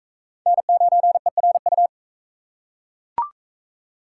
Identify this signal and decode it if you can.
Morse code, heard as a keyed tone, about 38 wpm, 700 Hz. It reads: N9ERU